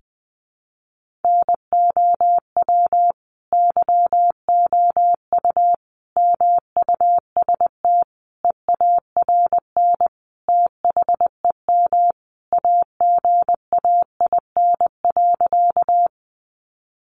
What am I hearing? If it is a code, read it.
Morse code, heard as a keyed tone, about 20 wpm, 700 Hz. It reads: NOW YOU MUST EARN THEM AGAIN.
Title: Sentences from Top 1500 Words 20wpm